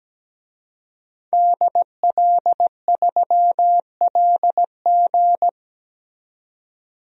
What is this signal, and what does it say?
Morse code, heard as a keyed tone, about 17 wpm, 700 Hz. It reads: DL3LG
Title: DXCC-100 Call Signs - 5 Characters - 17wpm